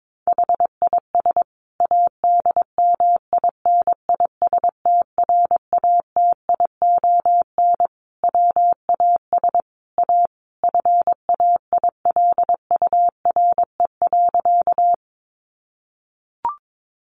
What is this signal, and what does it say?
Morse code, heard as a keyed tone, about 22 wpm, 700 Hz. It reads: HIS ADMINISTRATION WAS A FAILURE.
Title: Sentences from Top 1300 Words 22wpm